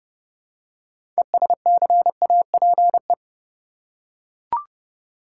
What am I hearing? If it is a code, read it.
Morse code, heard as a keyed tone, about 30 wpm, 700 Hz. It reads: ESCAPE
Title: Six Letter Words - 30wpm